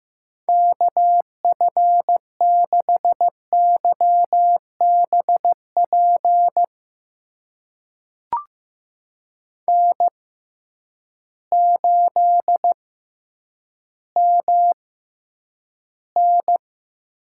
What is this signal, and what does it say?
Morse code, heard as a keyed tone, about 15 wpm, spaced as for 9 wpm, 700 Hz. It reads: KF6YBP N8MN
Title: USA Call Signs - 6x Character Spacing - 15wpm